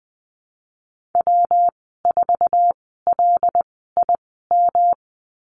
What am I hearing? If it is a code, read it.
Morse code, heard as a keyed tone, about 20 wpm, spaced as for 15 wpm, 700 Hz. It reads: W4LIM
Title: 147 Lesson 24g - Letter y - Call Signs 20wpm